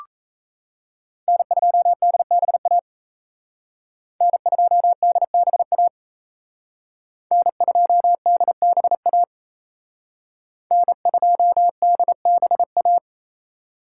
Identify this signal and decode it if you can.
Morse code, heard as a keyed tone, about 35 wpm, 700 Hz. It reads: N2DBA N2DBA N2DBA N2DBA